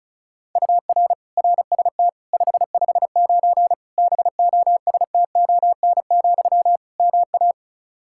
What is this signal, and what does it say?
Morse code, heard as a keyed tone, about 36 wpm, 700 Hz. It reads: UR RST 559 BOSTON, MA
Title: K1USN SST QSO Practice 35wpm